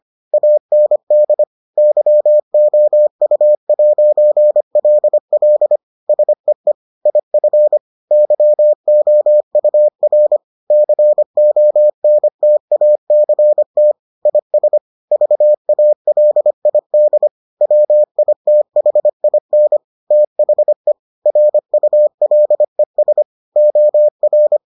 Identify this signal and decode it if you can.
Morse code, heard as a keyed tone, about 25 wpm, 600 Hz. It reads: AND YOU'LL SEE IF YOUR CONTACT IS VALID WITHIN THE RULES OR